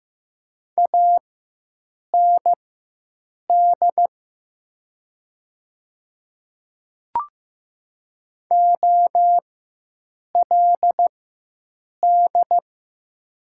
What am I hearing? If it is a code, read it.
Morse code, heard as a keyed tone, about 15 wpm, spaced as for 7 wpm, 700 Hz. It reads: AND OLD
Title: 4x Character Spacing - Top 200 Words - 15wpm